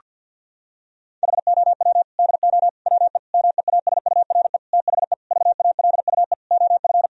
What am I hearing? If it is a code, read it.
Morse code, heard as a keyed tone, about 52 wpm, 700 Hz. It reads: HOW DO WE MEASURE THE VALUE OF
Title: Sentences from Top 1000 Words - Review Part 3 - 50wpm